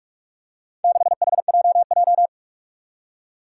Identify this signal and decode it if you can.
Morse code, heard as a keyed tone, about 46 wpm, 700 Hz. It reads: 6HJJ
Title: Four Challenging Letter-Number Combinations 45wpm